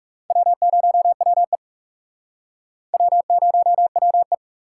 Morse code, heard as a keyed tone, about 40 wpm, 700 Hz. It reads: W0WE W0WE